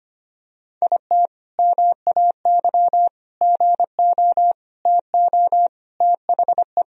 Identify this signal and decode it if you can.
Morse code, heard as a keyed tone, about 25 wpm, 700 Hz. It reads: IT MAY GO TO THE
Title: Sentences from Top 1000 Words - Review Part 1 - 25wpm